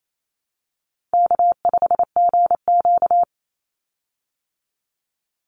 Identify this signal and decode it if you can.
Morse code, heard as a keyed tone, about 28 wpm, 700 Hz. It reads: K5GQ
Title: Easy Call Signs Repeated 3x - 28wpm